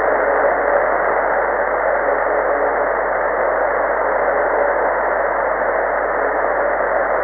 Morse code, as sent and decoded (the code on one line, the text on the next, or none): none